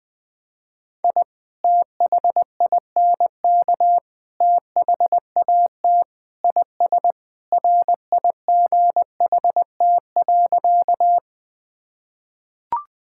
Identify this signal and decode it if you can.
Morse code, heard as a keyed tone, about 20 wpm, 700 Hz. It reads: I THINK THAT IS RIGHT.
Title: Sentences with 3-Word Phrases - 20wpm